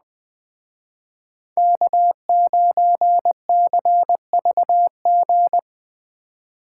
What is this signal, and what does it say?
Morse code, heard as a keyed tone, about 20 wpm, 700 Hz. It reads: K9CVG